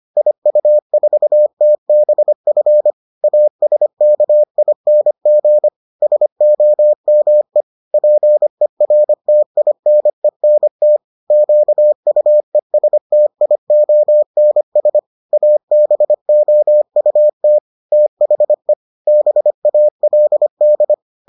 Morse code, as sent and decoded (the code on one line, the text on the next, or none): .. ..- ....- - -... ..-. / .- ... -.- .. -. --. / ... --- -- . / .--. .-. - .. -. -. - / --.- ..- . ... - .. --- -. ... / .- -... --- ..- - / - .... / -... .- .-.. -..
IU4TBF ASKING SOME PERTINENT QUESTIONS ABOUT THE BALD